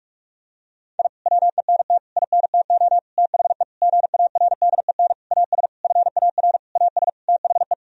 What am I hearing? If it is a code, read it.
Morse code, heard as a keyed tone, about 46 wpm, 700 Hz. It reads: I WENT INTO THE GARDEN AS FAR AS THE
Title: Sentences from Top 1000 Words - Review Part 4 - 45wpm